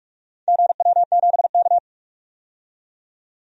Morse code, heard as a keyed tone, about 48 wpm, 700 Hz. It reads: GW7K
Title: DXCC-100 Call Signs - 4 Characters - 45wpm